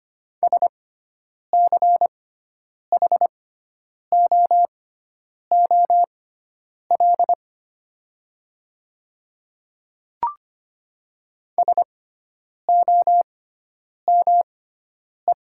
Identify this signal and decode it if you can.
Morse code, heard as a keyed tone, about 25 wpm, spaced as for 9 wpm, 700 Hz. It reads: SCHOOL SOME
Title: 6x Character Spacing - Top 300 Words - 25wpm